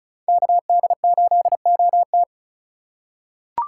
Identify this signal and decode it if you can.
Morse code, heard as a keyed tone, about 36 wpm, 700 Hz. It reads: KD8OT
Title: USA Call Signs - 4x Character Spacing - 35wpm